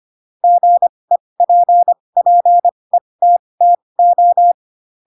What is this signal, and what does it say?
Morse code, heard as a keyed tone, about 25 wpm, spaced as for 20 wpm, 700 Hz. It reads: GEPPETTO
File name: Pinocchio / Pinocchio ch8 15 25 ik1hge